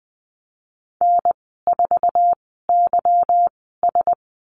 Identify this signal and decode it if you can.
Morse code, heard as a keyed tone, about 20 wpm, spaced as for 15 wpm, 700 Hz. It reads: N4YS
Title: 154 Lesson 25g - Letter p - Call Signs 20wpm